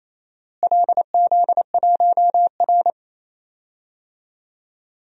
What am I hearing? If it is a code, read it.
Morse code, heard as a keyed tone, about 28 wpm, 700 Hz. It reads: LZ1R
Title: DXCC-100 Call Signs - 4 Characters - Repeated 2x - 28wpm